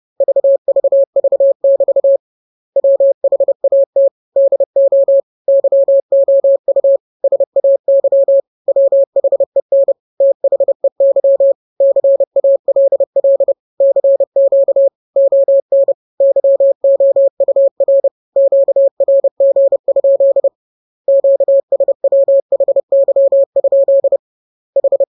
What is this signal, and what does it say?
Morse code, heard as a keyed tone, about 30 wpm, 550 Hz. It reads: VVV <BT> WHAT DO YOU SAY WHEN THEY CALL CQ ON YOUR QRG? QS WHY? H
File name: jokes / joke 21 30 0000